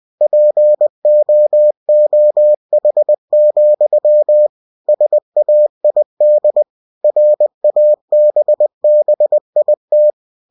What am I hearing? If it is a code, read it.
Morse code, heard as a keyed tone, about 20 wpm, 600 Hz. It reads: POOH, SAID RABBIT
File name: cw-books / winnie-the-pooh / winnie-20wpm-page-0063